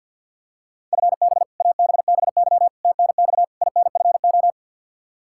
Morse code, heard as a keyed tone, about 52 wpm, 700 Hz. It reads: FB ABBY TNX INFO
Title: Practice QSO - Exchange 3 - More Info - 50wpm